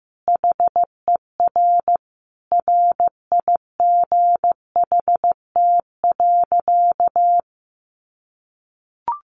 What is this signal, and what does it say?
Morse code, heard as a keyed tone, about 15 wpm, 700 Hz. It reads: HER RIGHT.
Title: Sentences from Top 1300 Words 15wpm